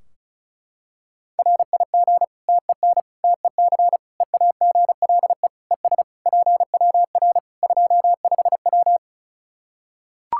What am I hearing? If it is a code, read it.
Morse code, heard as a keyed tone, about 36 wpm, 700 Hz. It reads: RIG TEN TEC EAGLE ES PWR 25W
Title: Practice QSO - Exchange 2 - Info - 35wpm